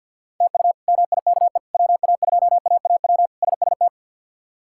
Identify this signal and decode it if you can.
Morse code, heard as a keyed tone, about 53 wpm, 700 Hz. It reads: TU MIKE WA2AAW SST
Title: K1USN SST QSO Practice - 50wpm